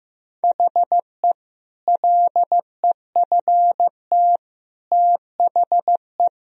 Morse code, heard as a keyed tone, about 15 wpm, 700 Hz. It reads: HE LEFT THE
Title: Sentences from Top 1000 Words - Review Part 1 - 15wpm